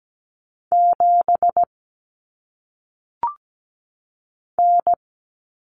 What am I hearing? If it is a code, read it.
Morse code, heard as a keyed tone, about 17 wpm, 700 Hz. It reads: 7 N
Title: Common Cut Numbers - 1-Digit 17wpm